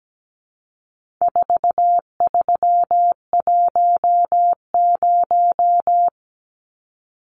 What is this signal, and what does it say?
Morse code, heard as a keyed tone, about 17 wpm, 700 Hz. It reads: 4310
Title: Numbers - 4-Digit 17wpm